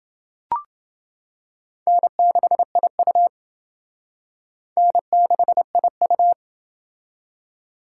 Morse code, heard as a keyed tone, about 28 wpm, 700 Hz. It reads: N6IU N6IU